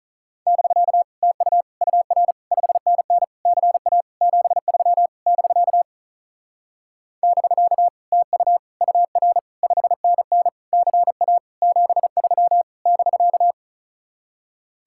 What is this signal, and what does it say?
Morse code, heard as a keyed tone, about 42 wpm, 700 Hz. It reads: <BK> TU UR 5NN CA 73 <BK> <BK> TU UR 5NN CA 73 <BK>
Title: Speed-Racing Common QSO Phrases - 35wpm